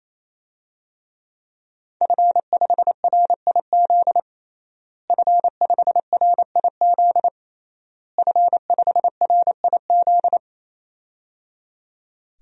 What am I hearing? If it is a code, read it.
Morse code, heard as a keyed tone, about 28 wpm, 700 Hz. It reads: F5RIZ F5RIZ F5RIZ
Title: DXCC-100 Call Signs - 5 Characters - Repeated 3x - 28wpm